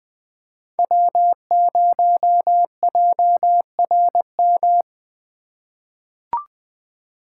Morse code, heard as a keyed tone, about 20 wpm, 700 Hz. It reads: W0JRM
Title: USA Call Signs - 5 Characters - 2x Character Spacing - 20wpm